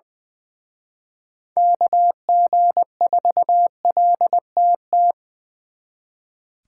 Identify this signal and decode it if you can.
Morse code, heard as a keyed tone, about 24 wpm, 700 Hz. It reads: KG4LTT
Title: Speed Racing - USA 2x3 Call Signs 20wpm